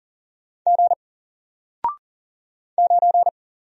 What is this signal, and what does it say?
Morse code, heard as a keyed tone, about 39 wpm, 700 Hz. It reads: G 9
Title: Single Character (Letter:Number:Punctuation:Prosign) 40wpm